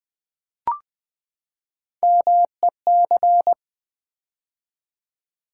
Morse code, heard as a keyed tone, about 20 wpm, 700 Hz. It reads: MEC
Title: Build A Word - 3 Letters At A Time - 8-Letter Words - 20wpm